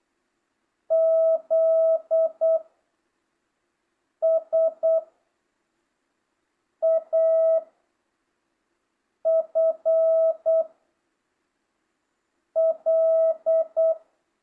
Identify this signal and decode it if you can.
Morse code, heard as a keyed tone, about 8 wpm, 650 Hz. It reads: Z S A F L